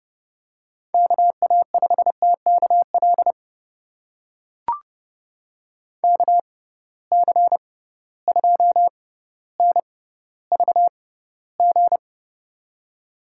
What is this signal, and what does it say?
Morse code, heard as a keyed tone, about 30 wpm, spaced as for 15 wpm, 700 Hz. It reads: KA5TKL KC2NVG
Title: USA Call Signs - 6 Characters - 6x Character Spacing - 30wpm